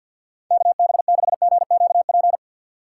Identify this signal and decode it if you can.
Morse code, heard as a keyed tone, about 54 wpm, 700 Hz. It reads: KB6GQP
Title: USA Call Signs - 6 Characters - 3x Character Spacing - 50wpm